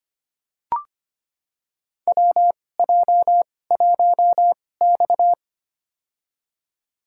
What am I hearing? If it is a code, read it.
Morse code, heard as a keyed tone, about 25 wpm, spaced as for 18 wpm, 700 Hz. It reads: WJ1X
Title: USA Call Signs - 4 Characters - 2x Character Spacing - 25wpm